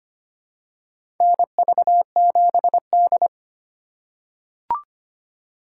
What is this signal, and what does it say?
Morse code, heard as a keyed tone, about 25 wpm, 700 Hz. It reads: NV7D
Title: USA Call Signs - 4 Characters - 3x Character Spacing - 25wpm